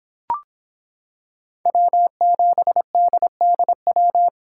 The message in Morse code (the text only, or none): W7DDW